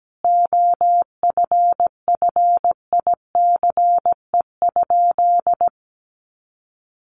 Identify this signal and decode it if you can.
Morse code, heard as a keyed tone, about 17 wpm, 700 Hz. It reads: OFFICE?
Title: Sentences from Top 1000 Words 17wpm